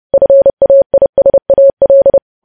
Morse code, heard as a keyed tone, about 30 wpm, 550 Hz. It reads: FAISAL
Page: news headlines in morse code at 30 wpm